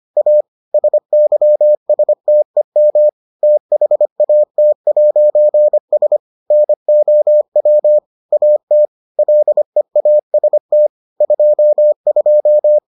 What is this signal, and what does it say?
Morse code, heard as a keyed tone, about 25 wpm, 600 Hz. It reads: A SYSTEM THAT'S NOW AT LEAST 22